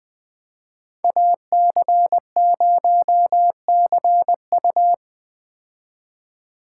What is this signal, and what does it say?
Morse code, heard as a keyed tone, about 20 wpm, 700 Hz. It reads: AC0CU
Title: 2023 POTA Hunter Call Signs - 20wpm